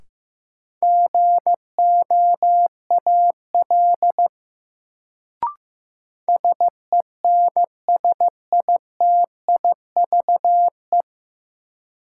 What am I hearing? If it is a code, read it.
Morse code, heard as a keyed tone, about 15 wpm, 700 Hz. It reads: GOAL SENSITIVE
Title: Sentences from Top 1400 Words - New Word Drill Alt - 15wpm